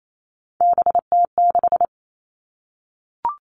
Morse code, heard as a keyed tone, about 28 wpm, 700 Hz. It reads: BT6
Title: All Cut Numbers - 3-Digit 28wpm